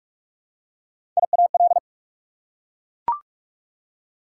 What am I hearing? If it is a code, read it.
Morse code, heard as a keyed tone, about 43 wpm, 690 Hz. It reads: IAL
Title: Prefixes and Suffixes 45wpm